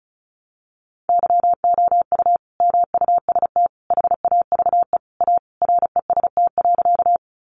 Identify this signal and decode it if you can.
Morse code, heard as a keyed tone, about 34 wpm, 700 Hz. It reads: YOU MUST HAVE A REST.
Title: Sentences from Top 500 Words 35wpm